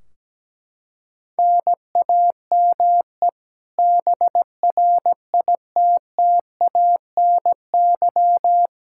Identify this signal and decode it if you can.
Morse code, heard as a keyed tone, about 17 wpm, 700 Hz. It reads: NAME BRITTANY
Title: Practice QSO - Exchange 1 - Beginning - 17wpm